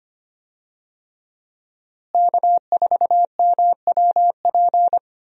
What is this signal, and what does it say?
Morse code, heard as a keyed tone, about 25 wpm, 700 Hz. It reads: K4MWP
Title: USA Call Signs - 5 Characters - Repeated 3x - 25wpm